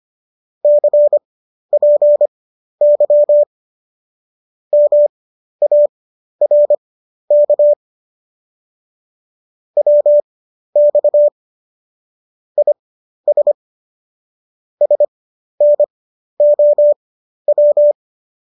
Morse code, heard as a keyed tone, about 25 wpm, spaced as for 12 wpm, 590 Hz. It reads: CPY MARK WX IS SNOW